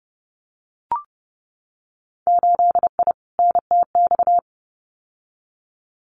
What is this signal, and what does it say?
Morse code, heard as a keyed tone, about 30 wpm, 700 Hz. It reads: 8I NTX
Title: Winter Field Day Exchanges Encore 30wpm